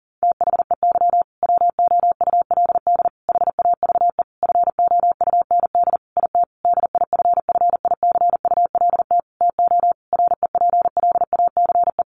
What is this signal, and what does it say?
Morse code, heard as a keyed tone, about 40 wpm, 700 Hz. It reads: THEY WOULD HAVE FOUND IT DIFFICULT TO REPLACE
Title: Sentences from Top 1400 Words 40wpm